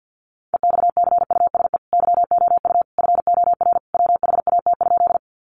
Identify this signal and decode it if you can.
Morse code, heard as a keyed tone, about 50 wpm, 700 Hz. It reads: EXCUSE YOU FOR WHAT?
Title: Sentences from Top 700 Words 50wpm